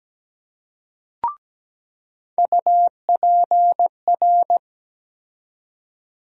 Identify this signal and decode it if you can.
Morse code, heard as a keyed tone, about 17 wpm, 700 Hz. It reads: UPR
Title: USA Call Sign Postfixes - 3 Letters 17wpm